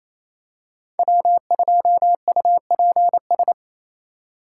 Morse code, heard as a keyed tone, about 28 wpm, 700 Hz. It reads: W2UPS
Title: USA Call Signs - 4x Character Spacing - 28wpm